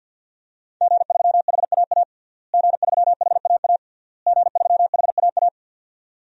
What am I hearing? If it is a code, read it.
Morse code, heard as a keyed tone, about 52 wpm, 700 Hz. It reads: G3HAA G3HAA G3HAA
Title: DXCC-100 Call Signs - 5 Characters - Repeated 3x - 50wpm